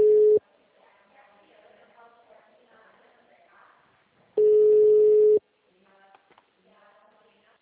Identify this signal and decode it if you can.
Ringback tone (Europe, 425 Hz), with speech.
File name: out- -2008- - - .87711